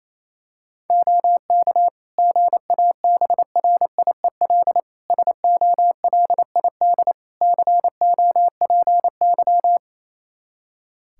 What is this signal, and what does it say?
Morse code, heard as a keyed tone, about 28 wpm, 700 Hz. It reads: OK GABRIEL SOLID COPY
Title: Practice QSO - Exchange 3 - More Info - 28wpm